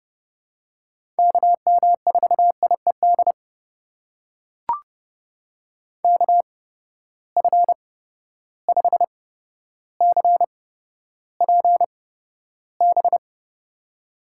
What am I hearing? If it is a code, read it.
Morse code, heard as a keyed tone, about 32 wpm, spaced as for 13 wpm, 700 Hz. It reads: KM4IED KF5CPB
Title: USA Call Signs - 6 Characters - 8x Character Spacing - 30wpm